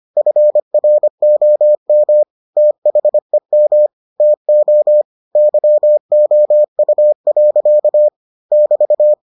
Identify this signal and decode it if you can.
Morse code, heard as a keyed tone, about 25 wpm, 600 Hz. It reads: FROM THEM TO YOU. <BT>